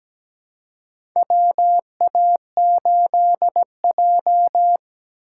Morse code, heard as a keyed tone, about 17 wpm, 700 Hz. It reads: WA8J
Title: USA Call Signs - 4 Characters - 3x Character Spacing - 17wpm